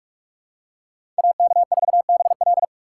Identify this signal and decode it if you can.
Morse code, heard as a keyed tone, about 44 wpm, 700 Hz. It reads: AK4BL
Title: USA Call Signs - 5 Characters - 4x Character Spacing - 45wpm